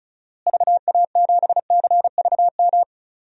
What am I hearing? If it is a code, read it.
Morse code, heard as a keyed tone, about 36 wpm, 700 Hz. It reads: VA7CVM
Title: DXCC-100 Call Signs - 6 Characters - Repeated 2x - 35wpm